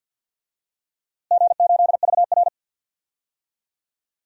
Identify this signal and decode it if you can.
Morse code, heard as a keyed tone, about 53 wpm, 700 Hz. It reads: G8VR
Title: DXCC-100 Call Signs - 4 Characters - Repeated 3x - 50wpm